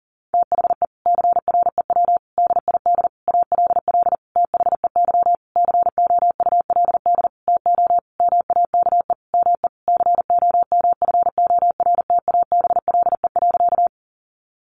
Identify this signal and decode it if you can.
Morse code, heard as a keyed tone, about 37 wpm, 700 Hz. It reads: THE CREW DID ALL THEY COULD TO MAKE ME COMFORTABLE.
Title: Sentences from Top 1500 Words 40wpm